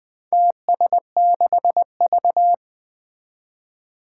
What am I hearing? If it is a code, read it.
Morse code, heard as a keyed tone, about 20 wpm, 700 Hz. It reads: TS6V
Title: Four Challenging Letter-Number Combinations 20wpm